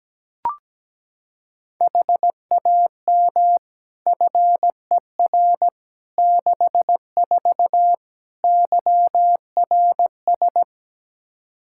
Morse code, heard as a keyed tone, about 17 wpm, 700 Hz. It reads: HAM FER 64 YRS
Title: K1USN SST QSO Practice 17wpm